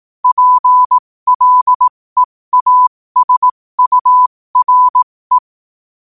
Morse code, heard as a keyed tone, about 18 wpm, spaced as for 15 wpm, 1000 Hz. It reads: PLEASURE
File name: qotdfiles / 15wpm